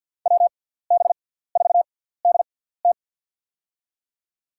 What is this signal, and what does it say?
Morse code, heard as a keyed tone, about 53 wpm, spaced as for 17 wpm, 700 Hz. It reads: WB4DT
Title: USA Call Signs - 5 Characters - 6x Character Spacing - 50wpm